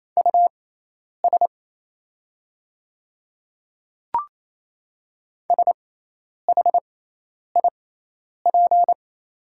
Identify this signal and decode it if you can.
Morse code, heard as a keyed tone, about 27 wpm, spaced as for 10 wpm, 690 Hz. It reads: US SHIP
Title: 6x Character Spacing - Top 500 Words - 28wpm